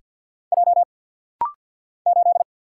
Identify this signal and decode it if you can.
Morse code, heard as a keyed tone, about 50 wpm, 700 Hz. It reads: J 8
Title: Single Letter-Number Alt - Rapid-Fire 50wpm